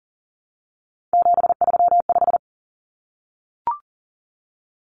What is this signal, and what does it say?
Morse code, heard as a keyed tone, about 40 wpm, 700 Hz. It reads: 735
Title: Common Cut Numbers - 3-Digit 40wpm